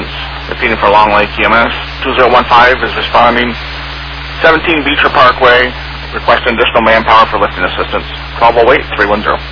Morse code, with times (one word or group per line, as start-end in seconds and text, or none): none